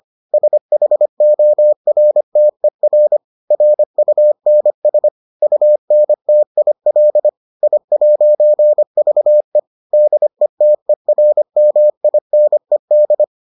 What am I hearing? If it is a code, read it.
Morse code, heard as a keyed tone, about 25 wpm, 600 Hz. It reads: SHORTER RUNS UNTIL I'VE DETERMINED